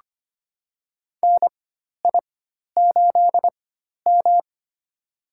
Morse code, heard as a keyed tone, about 25 wpm, spaced as for 12 wpm, 700 Hz. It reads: NI8M
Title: USA Call Signs - 4 Characters - 4x Character Spacing - 25wpm